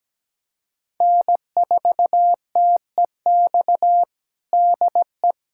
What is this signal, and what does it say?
Morse code, heard as a keyed tone, about 17 wpm, 700 Hz. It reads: N4TEX DE
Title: USA Callsign DE Callsign - 5 Characters - 17wpm